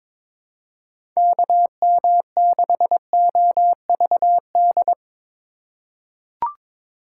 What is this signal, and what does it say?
Morse code, heard as a keyed tone, about 23 wpm, 700 Hz. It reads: KM6OVD
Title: USA Call Signs - 3x Character Spacing - 22wpm